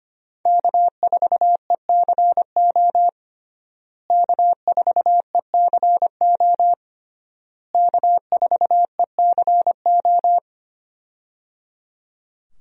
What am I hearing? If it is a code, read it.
Morse code, heard as a keyed tone, about 25 wpm, 700 Hz. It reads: K4ECO K4ECO K4ECO
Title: USA Call Signs - 5 Characters - Repeated 3x - 25wpm